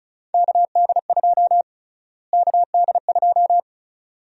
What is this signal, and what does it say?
Morse code, heard as a keyed tone, about 35 wpm, 700 Hz. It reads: KD2 KD2
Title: USA Call Sign Prefixes - Repeated 2x - 35wpm